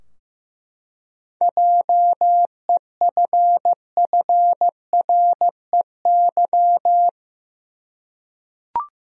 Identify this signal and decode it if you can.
Morse code, heard as a keyed tone, about 15 wpm, 700 Hz. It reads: JEFFREY
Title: USA Common Names - 15wpm